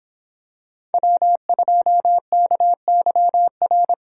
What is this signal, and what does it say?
Morse code, heard as a keyed tone, about 26 wpm, 700 Hz. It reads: W2KYR